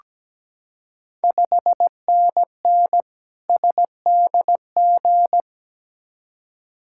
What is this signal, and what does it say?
Morse code, heard as a keyed tone, about 17 wpm, 700 Hz. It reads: 5NN SDG
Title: Ontario QSO Party Exchange 17wpm